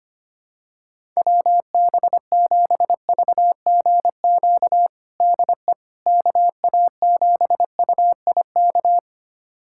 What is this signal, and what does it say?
Morse code, heard as a keyed tone, about 25 wpm, 700 Hz. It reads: WB7VGQ DE KA7UIK
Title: USA Callsign DE Callsign - 6 Characters - 25wpm